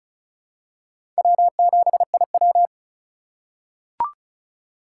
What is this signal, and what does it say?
Morse code, heard as a keyed tone, about 36 wpm, 700 Hz. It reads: W7IW
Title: USA Call Signs - 4 Characters - 4x Character Spacing - 35wpm